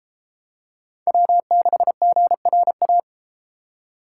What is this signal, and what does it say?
Morse code, heard as a keyed tone, about 29 wpm, 700 Hz. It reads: W6GRA